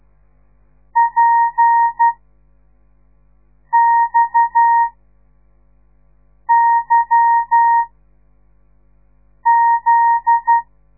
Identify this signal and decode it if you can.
Morse code, heard as a keyed tone, about 12 wpm, 930 Hz. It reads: P X Y Z
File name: radiomorningstar / morse / les 8